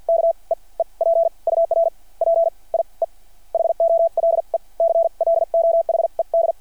Morse code, heard as a keyed tone, about 51 wpm, spaced as for 36 wpm, 670 Hz. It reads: KEEP UR PIE HOLE CLOHED
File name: archived / CRSnet